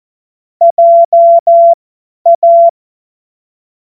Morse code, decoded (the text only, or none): JA